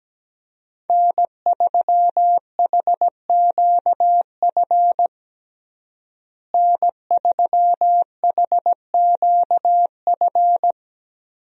Morse code, heard as a keyed tone, about 17 wpm, 700 Hz. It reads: N3HQF N3HQF
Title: USA Call Signs - 5 Characters - Repeated 2x - 17wpm